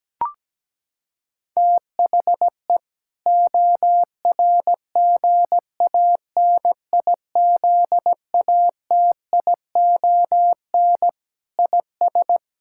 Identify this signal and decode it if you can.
Morse code, heard as a keyed tone, about 17 wpm, 700 Hz. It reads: THE ORGANIZATION IS
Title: Sentences from Top 1000 Words - Review Part 5 - 17wpm